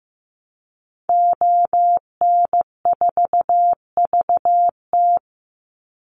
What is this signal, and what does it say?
Morse code, heard as a keyed tone, about 15 wpm, 700 Hz. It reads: ON4VT
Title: Call Signs Repeated 3x - 15wpm